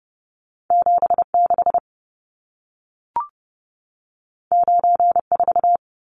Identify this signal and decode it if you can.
Morse code, heard as a keyed tone, about 30 wpm, 700 Hz. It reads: 76 94
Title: Numbers - 2-Digit 30wpm